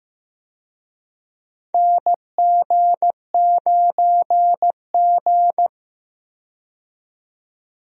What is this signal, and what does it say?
Morse code, heard as a keyed tone, about 15 wpm, 700 Hz. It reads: NG9G
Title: USA Call Signs - 4 Characters - 15wpm